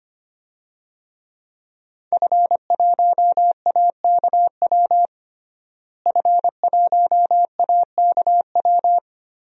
Morse code, heard as a keyed tone, about 25 wpm, 700 Hz. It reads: F1AKW F1AKW
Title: DXCC-100 Call Signs - 5 Characters - Repeated 3x - 25wpm